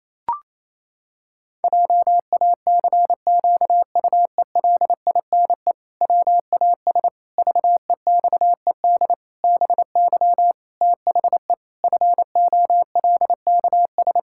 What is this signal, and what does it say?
Morse code, heard as a keyed tone, about 28 wpm, 700 Hz. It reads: JACQUELINE WAS VEXED BY THE FOLKS
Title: Pangram Challenge v2 - 28wpm